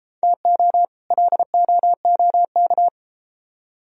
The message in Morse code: TO LOOK

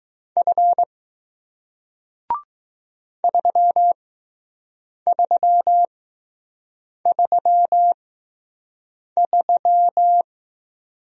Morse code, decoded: F 3333

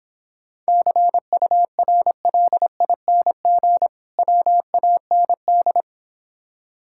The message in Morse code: CURLING WAND

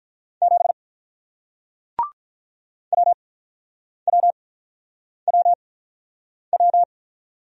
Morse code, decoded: 7 WWWW

